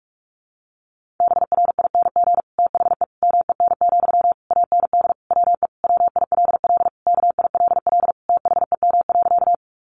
Morse code, heard as a keyed tone, about 44 wpm, 700 Hz. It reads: BRING THE MEN, AND WE WILL KILL THEM.
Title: Sentences from Top 1400 Words 45wpm